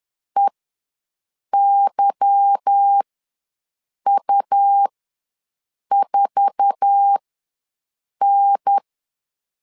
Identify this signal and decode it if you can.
Morse code, heard as a keyed tone, about 10 wpm, spaced as for 6 wpm, 800 Hz. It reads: EYU4N
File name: LEZIONI / 10 SIGNS / 10D COMA ws 10wpm 4314